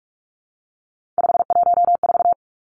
Morse code, heard as a keyed tone, about 46 wpm, 700 Hz. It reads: 514
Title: Three Challenging Letter-Number Combinations 45wpm